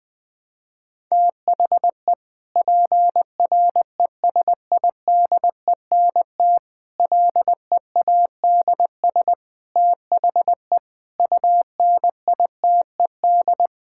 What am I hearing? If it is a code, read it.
Morse code, heard as a keyed tone, about 21 wpm, 700 Hz. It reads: THE PRESIDENT LEADS THE UNITED
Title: Sentences with 5-Word Phrases - 20wpm